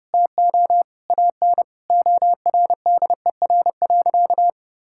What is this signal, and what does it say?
Morse code, heard as a keyed tone, about 30 wpm, 700 Hz. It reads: TO AN ORDER.
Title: Sentences from Top 1600 Words 30wpm